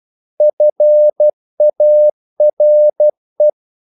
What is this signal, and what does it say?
Morse code, heard as a keyed tone, about 12 wpm, 600 Hz. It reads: FARE
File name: Korte ord DK 12wpm